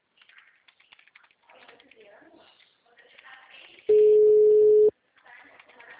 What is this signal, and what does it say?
Ringback tone (Europe, 425 Hz), with speech.